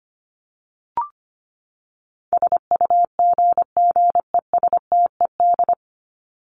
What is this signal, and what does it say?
Morse code, heard as a keyed tone, about 25 wpm, 700 Hz. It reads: SUGGESTED
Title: ED Suffix - Top 100 Words with Base Words 25wpm